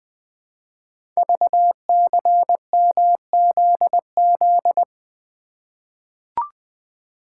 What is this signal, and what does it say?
Morse code, heard as a keyed tone, about 20 wpm, 700 Hz. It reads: VCMZZ
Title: Code Groups - 5 Letters - 20wpm